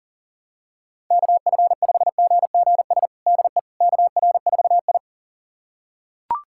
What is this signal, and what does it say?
Morse code, heard as a keyed tone, about 40 wpm, 700 Hz. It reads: KF5GGS DE KR4I
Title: USA Callsign DE Callsign 40wpm